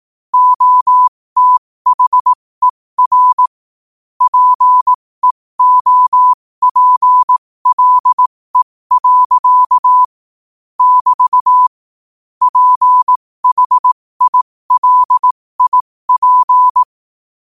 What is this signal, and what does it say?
Morse code, heard as a keyed tone, about 18 wpm, spaced as for 15 wpm, 1000 Hz. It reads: OTHER PEOPLE. <BT> PHILIP
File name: qotdfiles / 15wpm